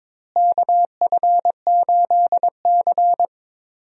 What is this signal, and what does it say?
Morse code, heard as a keyed tone, about 22 wpm, 700 Hz. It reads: KF8C
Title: USA Call Signs - 4 Characters - Repeated 3x - 22wpm